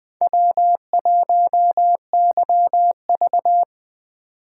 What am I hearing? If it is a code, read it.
Morse code, heard as a keyed tone, about 20 wpm, 700 Hz. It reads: W1YV
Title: Speed Racing - USA 1x2 Call Signs 17wpm